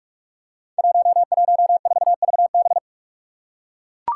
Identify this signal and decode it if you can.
Morse code, heard as a keyed tone, about 44 wpm, 700 Hz. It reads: 114VB
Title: Five Challenging Letter-Number Combinations 45wpm